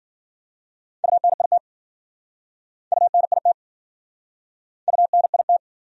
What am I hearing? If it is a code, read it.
Morse code, heard as a keyed tone, about 52 wpm, 700 Hz. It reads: UNIT UNIT UNIT